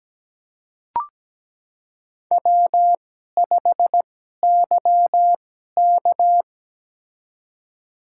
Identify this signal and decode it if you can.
Morse code, heard as a keyed tone, about 17 wpm, spaced as for 12 wpm, 700 Hz. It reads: W5YK
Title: USA Call Signs - 4 Characters - 2x Character Spacing - 17wpm